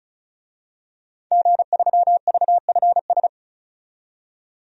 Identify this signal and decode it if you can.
Morse code, heard as a keyed tone, about 35 wpm, 700 Hz. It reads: G3VFS
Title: DXCC-100 Call Signs - 5 Characters - 35wpm